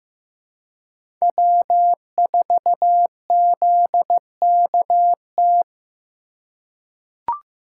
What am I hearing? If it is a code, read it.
Morse code, heard as a keyed tone, about 15 wpm, 700 Hz. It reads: W4ZKT
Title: USA Call Signs - 3x Character Spacing - 15wpm